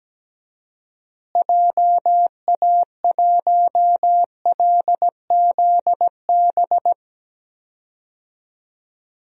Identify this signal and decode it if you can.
Morse code, heard as a keyed tone, about 17 wpm, 700 Hz. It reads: JA1LZB
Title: DXCC-100 Call Signs - 6 Characters - Repeated 2x - 17wpm